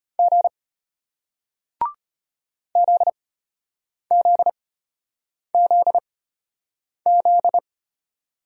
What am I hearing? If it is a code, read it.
Morse code, heard as a keyed tone, about 33 wpm, spaced as for 8 wpm, 700 Hz. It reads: G ZZZZ